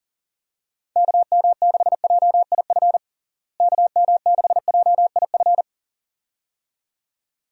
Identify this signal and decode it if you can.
Morse code, heard as a keyed tone, about 37 wpm, 700 Hz. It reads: KM6JIF KM6JIF
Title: USA Call Signs - 6 Characters - Repeated 2x - 40wpm